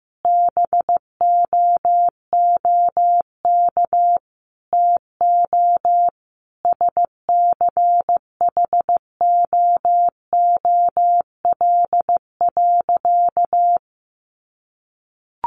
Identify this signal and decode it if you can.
Morse code, heard as a keyed tone, about 15 wpm, 700 Hz. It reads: BOOK TO SCHOOL.
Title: Sentences from Top 1400 Words 15wpm